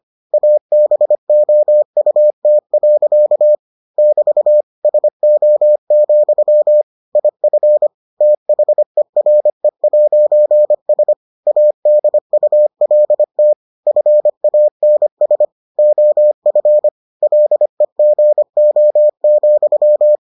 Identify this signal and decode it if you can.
Morse code, heard as a keyed tone, about 25 wpm, 600 Hz. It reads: ABOUT. <BT> SO, IF THERE'S ADULT FANS OF LEGO,